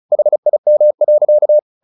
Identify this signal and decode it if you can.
Morse code, heard as a keyed tone, about 34 wpm, 600 Hz. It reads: HIM.
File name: the old man and the sea 2 35wpm